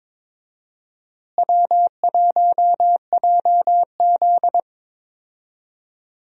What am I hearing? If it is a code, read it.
Morse code, heard as a keyed tone, about 21 wpm, 700 Hz. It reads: W1JZ
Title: USA Call Signs - 4 Characters - Repeated 3x - 22wpm